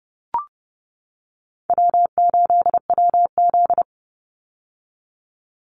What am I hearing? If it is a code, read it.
Morse code, heard as a keyed tone, about 30 wpm, 700 Hz. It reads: W8WZ
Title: Call Signs - 30wpm